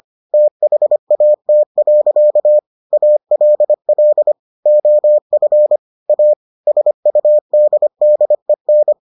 Morse code, heard as a keyed tone, about 25 wpm, 600 Hz. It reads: THAT. ALL OF A SUDDEN